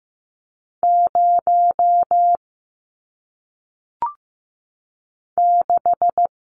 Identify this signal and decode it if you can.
Morse code, heard as a keyed tone, about 15 wpm, 700 Hz. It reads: 0 6
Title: Numbers - 1-Digit 15wpm